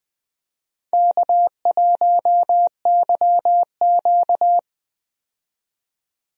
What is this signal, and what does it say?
Morse code, heard as a keyed tone, about 20 wpm, 700 Hz. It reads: K1YQ
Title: USA Call Signs - 4 Characters - 20wpm